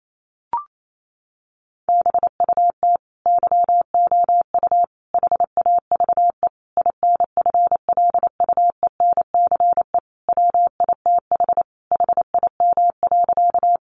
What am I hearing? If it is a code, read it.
Morse code, heard as a keyed tone, about 28 wpm, 700 Hz. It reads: BUT YOU HAVE INFLUENCE WITH HIM.
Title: Sentences from Top 1100 Words 28wpm